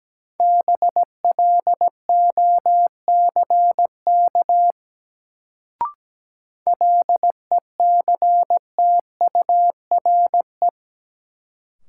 Morse code, heard as a keyed tone, about 17 wpm, 700 Hz. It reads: BLOCK LECTURE
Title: Sentences from Top 1600 Words - New Word Drill Alt - 17wpm